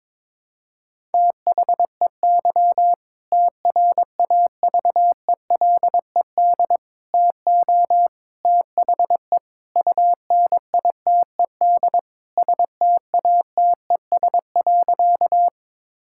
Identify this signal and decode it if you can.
Morse code, heard as a keyed tone, about 22 wpm, 700 Hz. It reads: THEY TRAVELED TO THE UNITED STATES.
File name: practice / 22wpm / Sentences with 2-Word Phrases - 22wpm